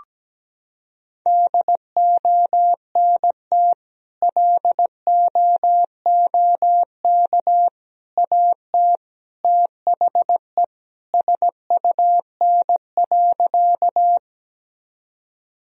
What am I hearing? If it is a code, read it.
Morse code, heard as a keyed tone, about 17 wpm, 700 Hz. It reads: DONT LOOK AT THE SUN.
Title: Sentences from Top 1000 Words - Review Part 1 - 17wpm